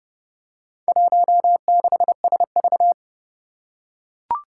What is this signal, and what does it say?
Morse code, heard as a keyed tone, about 30 wpm, 700 Hz. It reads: 16SV
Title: Four Challenging Letter-Number Combinations 30wpm